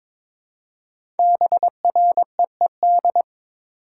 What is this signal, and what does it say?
Morse code, heard as a keyed tone, about 22 wpm, 700 Hz. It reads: BREED
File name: 5 Letter Words - First 3 Letters Given - 22wpm